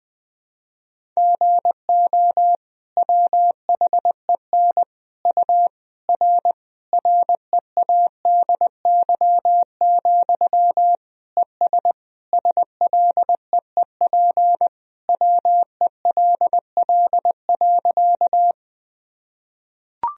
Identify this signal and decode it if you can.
Morse code, heard as a keyed tone, about 21 wpm, 700 Hz. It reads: GO WHEN U R READY, ES SLEEP WELL.
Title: Sentences from Top 500 Words and Abbreviations 20wpm